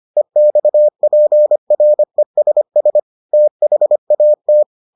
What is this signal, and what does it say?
Morse code, heard as a keyed tone, about 25 wpm, 600 Hz. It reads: EXPRESS THAT